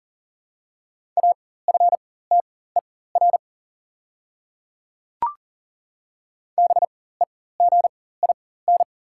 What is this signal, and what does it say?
Morse code, heard as a keyed tone, about 37 wpm, spaced as for 18 wpm, 690 Hz. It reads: AFTER BEGIN
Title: 4x Character Spacing - Top 500 Words - 40wpm